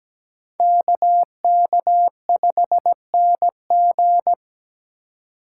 Morse code, heard as a keyed tone, about 20 wpm, 700 Hz. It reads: KK5NG
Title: Speed Racing - USA 2x2 Call Signs 17wpm